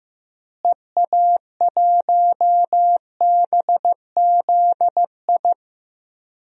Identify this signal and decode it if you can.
Morse code, heard as a keyed tone, about 15 wpm, 700 Hz. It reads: EA1BZI
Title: DXCC-100 Call Signs - 6 Characters - Repeated 3x - 15wpm